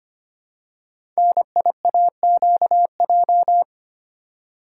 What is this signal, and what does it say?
Morse code, heard as a keyed tone, about 25 wpm, 700 Hz. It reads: NIAQJ
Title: Code Groups - 5 Letters - 25wpm